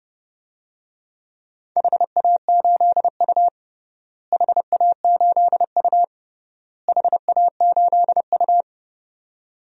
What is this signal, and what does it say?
Morse code, heard as a keyed tone, about 32 wpm, 700 Hz. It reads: HA8U HA8U HA8U
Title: DXCC-100 Call Signs - 4 Characters - Repeated 3x - 30wpm